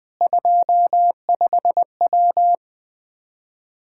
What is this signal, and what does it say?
Morse code, heard as a keyed tone, about 21 wpm, 700 Hz. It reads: 25W
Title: Practice QSO - Exchange 2 - Info - 20wpm